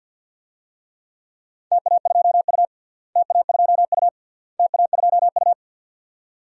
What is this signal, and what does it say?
Morse code, heard as a keyed tone, about 52 wpm, 700 Hz. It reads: TA2U TA2U TA2U
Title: DXCC-100 Call Signs - 4 Characters - Repeated 3x - 50wpm